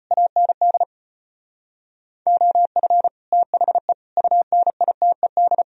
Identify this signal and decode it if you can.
Morse code, heard as a keyed tone, about 30 wpm, 700 Hz. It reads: AND OF THE UNITED
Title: Speed-Racing - Sets of 5 Words 25wpm